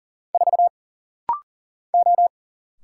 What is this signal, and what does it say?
Morse code, heard as a keyed tone, about 40 wpm, 700 Hz. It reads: 4 O
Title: Single Letter-Number Alt - Rapid-Fire 40wpm